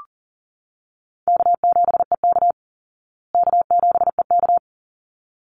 Morse code, heard as a keyed tone, about 40 wpm, 700 Hz. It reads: K7EK K7EK
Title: Easy Call Signs Repeated 2x - 40wpm